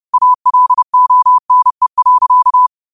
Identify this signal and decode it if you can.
Morse code, heard as a keyed tone, about 30 wpm, 1000 Hz. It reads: ALONE.